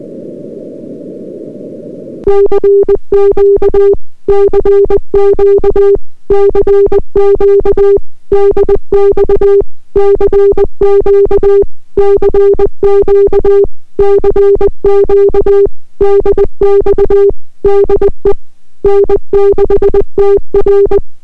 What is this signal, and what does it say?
Morse code, heard as a keyed tone, about 20 wpm, 400 Hz. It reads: CQ CQ CQ DX CQ CQ CQ DX DE N6TR